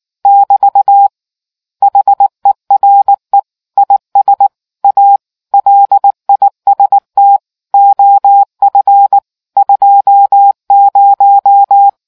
Morse code, heard as a keyed tone, about 19 wpm, 800 Hz. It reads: <BT> HERE IS A LIST OF 20